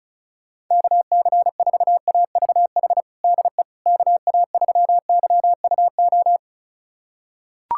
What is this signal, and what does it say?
Morse code, heard as a keyed tone, about 36 wpm, 700 Hz. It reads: KC4AVH DE KA3YUO
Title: USA Callsign DE Callsign - 6 Characters - 35wpm